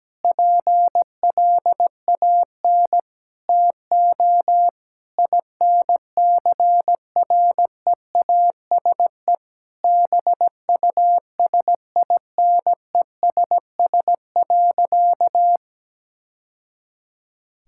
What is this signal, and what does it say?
Morse code, heard as a keyed tone, about 17 wpm, 700 Hz. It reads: PLAN TO INCREASE BUSINESS.
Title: Sentences from Top 1000 Words - Review Part 6 - 17wpm